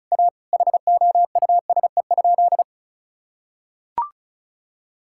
Morse code, heard as a keyed tone, about 34 wpm, 700 Hz. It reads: A HOUSE?
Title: Sentences from Top 1000 Words - Review Part 6 - 35wpm